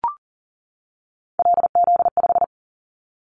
Morse code, heard as a keyed tone, about 40 wpm, 700 Hz. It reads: LZ5
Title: Callsign Prefixes for Top 100 DXCC - 40wpm